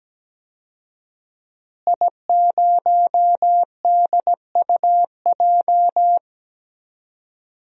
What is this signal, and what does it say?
Morse code, heard as a keyed tone, about 17 wpm, 700 Hz. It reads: I0DUJ
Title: DXCC-100 Call Signs - 5 Characters - Repeated 2x - 17wpm